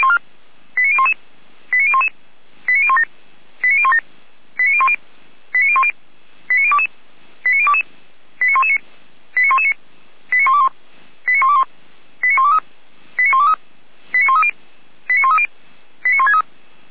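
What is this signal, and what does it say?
Five-tone selective call (ZVEI), heard as quick runs of five tones, tones of about 70 ms. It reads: 89011 89011 89017 89017 89010 89010 89022 89022 89119 89119 89121 89121 89123 89123 89130 89130 89163
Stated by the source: ZVEI five-tone call received by SDR, from the audio server of the Orlamünde fire brigade (feuerwehr-orlamuende).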